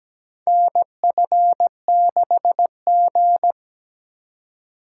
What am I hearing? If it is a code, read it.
Morse code, heard as a keyed tone, about 17 wpm, 700 Hz. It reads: NF6G
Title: USA Call Signs - 4x Character Spacing - 17wpm